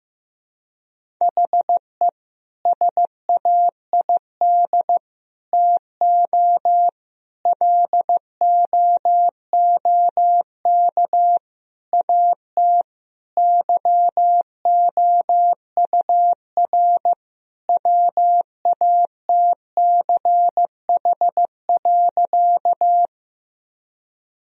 Morse code, heard as a keyed tone, about 15 wpm, 700 Hz. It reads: HE SAID TO LOOK AT YOUR WATCH.
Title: Sentences from Top 1000 Words - Review Part 1 - 15wpm